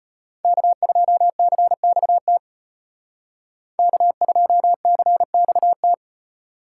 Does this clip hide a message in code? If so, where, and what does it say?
Yes: Morse code from 0.4 to 5.9 s, heard as a keyed tone, about 33 wpm, 700 Hz. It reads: K2CXT K2CXT